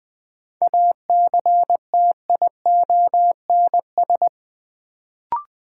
Morse code, heard as a keyed tone, about 20 wpm, 700 Hz. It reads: ACTIONS
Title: Sentences from Top 1000 Words - Review Part 6 - New Word Drill - 20wpm